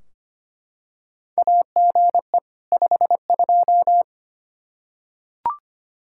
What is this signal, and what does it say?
Morse code, heard as a keyed tone, about 25 wpm, 700 Hz. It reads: AGE 52
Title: Practice QSO - Exchange 3 - More Info - 25wpm